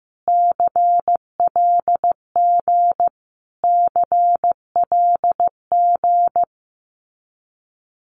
Morse code, heard as a keyed tone, about 15 wpm, 700 Hz. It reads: CLG CLG
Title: Abbreviations Repeated 3x - 15wpm